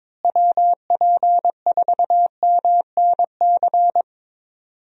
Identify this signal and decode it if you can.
Morse code, heard as a keyed tone, about 22 wpm, 700 Hz. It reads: WP4MNC
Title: USA Call Signs - 6 Characters - Repeated 2x - 22wpm